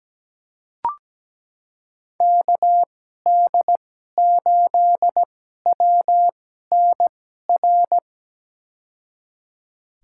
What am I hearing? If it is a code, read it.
Morse code, heard as a keyed tone, about 17 wpm, spaced as for 12 wpm, 700 Hz. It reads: KD8WNR
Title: USA Call Signs - 6 Characters - 2x Character Spacing - 17wpm